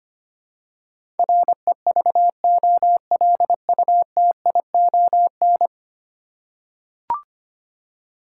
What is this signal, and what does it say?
Morse code, heard as a keyed tone, about 25 wpm, 700 Hz. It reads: REVOLUTION
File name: 10 Letter Words - First 3 Letters Given - 25wpm